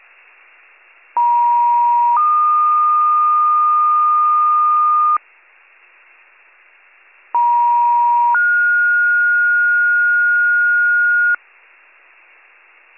Two-tone page, heard as a one-second tone, then a three-second tone, about 950 then 1220 Hz, about 950 then 1430 Hz. It is heard from 1.2 to 11.4 s.